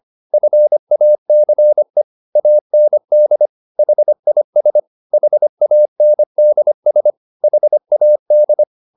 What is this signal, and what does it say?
Morse code, heard as a keyed tone, about 25 wpm, 600 Hz. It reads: FACE AND HIS HANDS HAD
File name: the old man and the sea part 1 25wpm